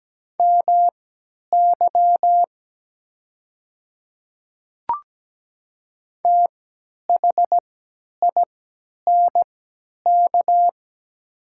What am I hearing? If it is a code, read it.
Morse code, heard as a keyed tone, about 17 wpm, spaced as for 10 wpm, 700 Hz. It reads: MY THINK
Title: 3x Character Spacing - Top 300 Words - 17wpm